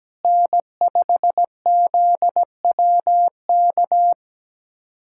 Morse code, heard as a keyed tone, about 17 wpm, 700 Hz. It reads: N5ZWK
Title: USA Callsign DE Callsign - 5 Characters - 17wpm